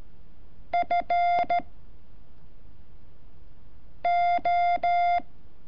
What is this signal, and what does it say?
Morse code, heard as a keyed tone, about 13 wpm, 700 Hz. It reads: F O
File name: curso / ejercicio 30